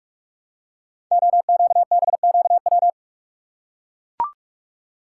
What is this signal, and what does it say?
Morse code, heard as a keyed tone, about 44 wpm, 690 Hz. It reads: OQBQW
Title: Code Groups - 5 Letters - 45wpm